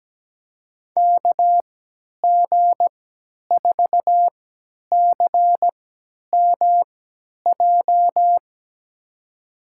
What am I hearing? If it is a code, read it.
Morse code, heard as a keyed tone, about 17 wpm, spaced as for 10 wpm, 700 Hz. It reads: KG4CMJ
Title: USA Call Signs - 6 Characters - 3x Character Spacing - 17wpm